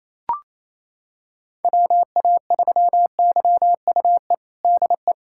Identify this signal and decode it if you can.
Morse code, heard as a keyed tone, about 28 wpm, 700 Hz. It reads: WA3YUE DE
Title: USA Callsign DE Callsign 28wpm